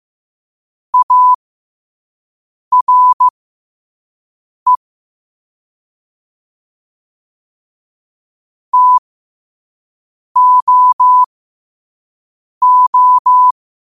Morse code, heard as a keyed tone, about 15 wpm, spaced as for 5 wpm, 1000 Hz. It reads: ARE TOO
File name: 2026-02-15-5wpm